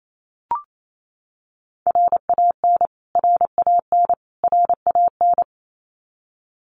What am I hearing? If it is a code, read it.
Morse code, heard as a keyed tone, about 28 wpm, 700 Hz. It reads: RAN RAN RAN
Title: Top 500 Words Repeated 28wpm